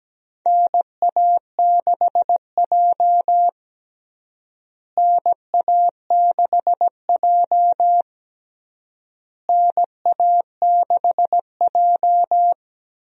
Morse code, heard as a keyed tone, about 17 wpm, 700 Hz. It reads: NA6J NA6J NA6J
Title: USA Call Signs - 4 Characters - Repeated 3x - 17wpm